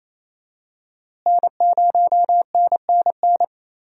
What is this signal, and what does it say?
Morse code, heard as a keyed tone, about 27 wpm, 700 Hz. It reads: N0NNN
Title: USA Call Signs - 5 Characters - Repeated 3x - 28wpm